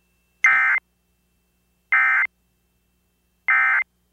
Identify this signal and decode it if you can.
Emergency Alert System (SAME) tones, heard as harsh buzzing bursts. It is heard from 0.4 to 3.8 s.